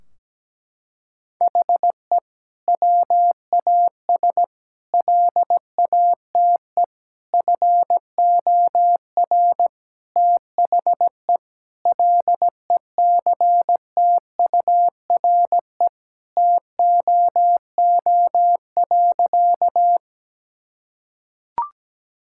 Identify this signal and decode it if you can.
Morse code, heard as a keyed tone, about 17 wpm, 700 Hz. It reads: HE WAS LATE FOR THE LECTURE TOO.
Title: Sentences from Top 1600 Words 17wpm